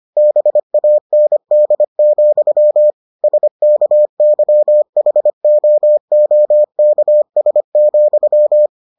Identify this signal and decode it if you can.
Morse code, heard as a keyed tone, about 25 wpm, 600 Hz. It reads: BAND, SKYHOOKS,